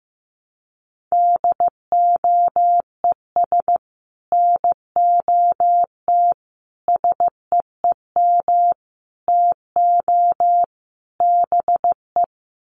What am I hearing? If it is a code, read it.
Morse code, heard as a keyed tone, about 15 wpm, 700 Hz. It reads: DOES NOT SEEM TO BE
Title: Sets of 5 Words 15wpm